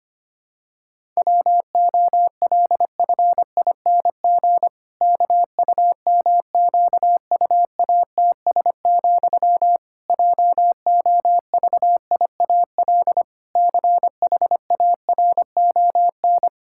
Morse code, heard as a keyed tone, about 25 wpm, 700 Hz. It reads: WOLFING KUMQUATS, JOVIAL CHARON
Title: Pangram Challenge v2 - 25wpm